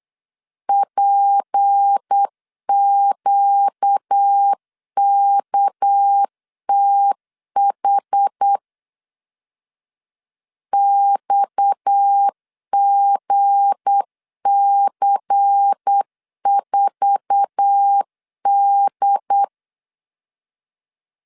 Morse code, heard as a keyed tone, about 8 wpm, 800 Hz. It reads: PQKTH XGC4D
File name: LEZIONI / 8 YJ45 / 8 YJ4 8wpm 3246